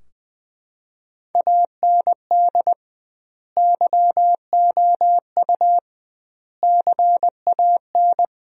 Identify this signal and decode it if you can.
Morse code, heard as a keyed tone, about 20 wpm, 700 Hz. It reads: AND YOU CAN
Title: 2x Word-Spacing - Sentences from Top 400 Words - 20wpm